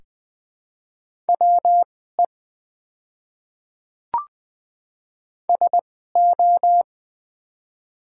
Morse code, heard as a keyed tone, about 20 wpm, spaced as for 14 wpm, 700 Hz. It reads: WE SO
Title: 2x Character Spacing - Top 100 Words - 20wpm